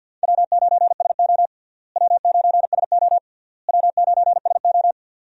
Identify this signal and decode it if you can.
Morse code, heard as a keyed tone, about 50 wpm, 700 Hz. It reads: W9SO W9SO W9SO
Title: USA Call Signs - 4 Characters - Repeated 3x - 50wpm